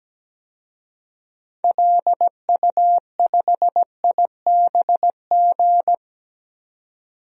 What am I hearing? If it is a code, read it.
Morse code, heard as a keyed tone, about 17 wpm, 700 Hz. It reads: LU5IBG
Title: DXCC-100 Call Signs - 6 Characters - 17wpm